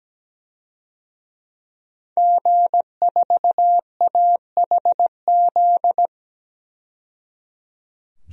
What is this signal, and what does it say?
Morse code, heard as a keyed tone, about 17 wpm, 700 Hz. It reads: G4AHZ
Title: DXCC-100 Call Signs - 5 Characters - 17wpm